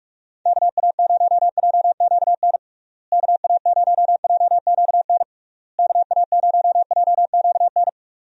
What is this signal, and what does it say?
Morse code, heard as a keyed tone, about 44 wpm, 700 Hz. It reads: KA0JQN KA0JQN KA0JQN
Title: USA Call Signs - 6 Characters - Repeated 3x - 45wpm